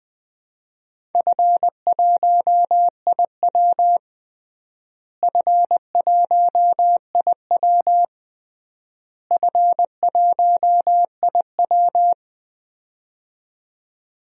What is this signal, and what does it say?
Morse code, heard as a keyed tone, about 20 wpm, 700 Hz. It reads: F1IW F1IW F1IW
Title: DXCC-100 Call Signs - 4 Characters - Repeated 3x - 20wpm